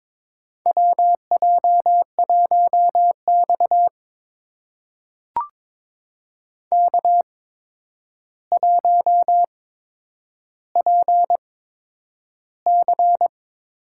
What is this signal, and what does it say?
Morse code, heard as a keyed tone, about 22 wpm, spaced as for 9 wpm, 700 Hz. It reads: WJ1X K1PC
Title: USA Call Signs - 4 Characters - 8x Character Spacing - 22wpm